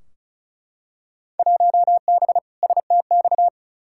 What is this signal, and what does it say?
Morse code, heard as a keyed tone, about 35 wpm, 700 Hz. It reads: 1B STX
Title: Field Day Exchanges Encore - 35wpm